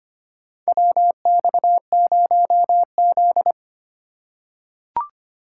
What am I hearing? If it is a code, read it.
Morse code, heard as a keyed tone, about 25 wpm, 700 Hz. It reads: WX0Z
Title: K1USN SST QSO Practice - 25wpm